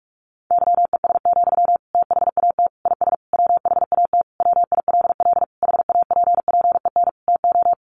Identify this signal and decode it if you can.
Morse code, heard as a keyed tone, about 44 wpm, 700 Hz. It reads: YES, THAT IS WHAT WILL HAPPEN TO
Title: Sentences from Top 500 Words 45wpm